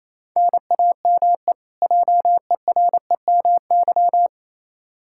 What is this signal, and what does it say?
Morse code, heard as a keyed tone, about 28 wpm, 700 Hz. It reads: NAME JEREMY
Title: K1USN SST QSO Practice 28wpm